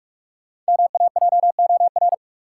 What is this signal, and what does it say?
Morse code, heard as a keyed tone, about 46 wpm, 700 Hz. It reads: MAJOR
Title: Sentences from Top 600 Words - New Word Drill - 45wpm